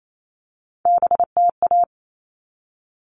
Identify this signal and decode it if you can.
Morse code, heard as a keyed tone, about 28 wpm, 700 Hz. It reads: BTA
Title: All Cut Numbers - 3-Digit 28wpm